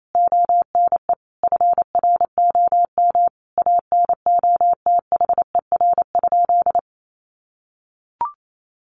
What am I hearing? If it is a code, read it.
Morse code, heard as a keyed tone, about 28 wpm, 700 Hz. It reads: ONE FROM ANOTHER?
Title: Sentences from Top 1400 Words 28wpm